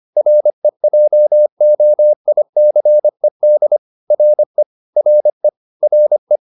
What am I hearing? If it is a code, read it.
Morse code, heard as a keyed tone, about 25 wpm, 600 Hz. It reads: REJOICED RE RE RE